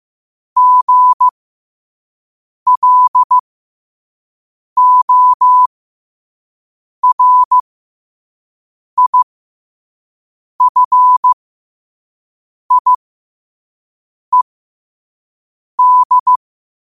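Morse code, heard as a keyed tone, about 15 wpm, spaced as for 5 wpm, 1000 Hz. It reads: GLORIFIED
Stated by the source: Morse code sent at 5 wpm for learning.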